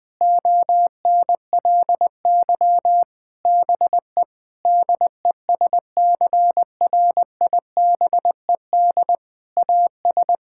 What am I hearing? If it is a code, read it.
Morse code, heard as a keyed tone, about 20 wpm, 700 Hz. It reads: ONLY BE DESCRIBED AS
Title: Sets of 8 Words - 20wpm